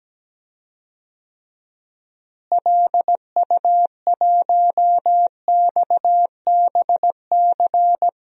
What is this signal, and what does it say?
Morse code, heard as a keyed tone, about 17 wpm, 700 Hz. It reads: LU1XBC
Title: DXCC-100 Call Signs - 6 Characters - Repeated 3x - 17wpm